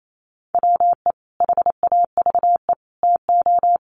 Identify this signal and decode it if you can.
Morse code, heard as a keyed tone, about 28 wpm, 700 Hz. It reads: WE HAVE TO
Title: Sets of 3 Words 28wpm